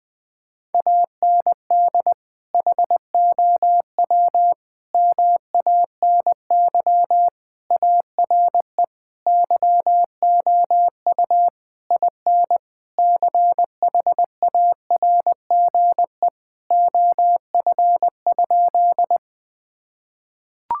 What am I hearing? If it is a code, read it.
Morse code, heard as a keyed tone, about 20 wpm, 700 Hz. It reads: AND HOW MANY ARE YOU IN CHARGE OF?
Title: Sentences from Top 1000 Words - Review Part 6 - 20wpm